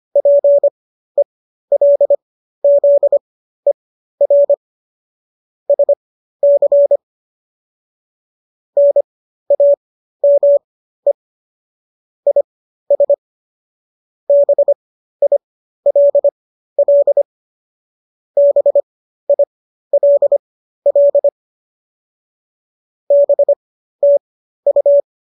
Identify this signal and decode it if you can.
Morse code, heard as a keyed tone, about 25 wpm, spaced as for 13 wpm, 570 Hz. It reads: PELZER SC NAME IS BILL BILL BTU
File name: qso206 13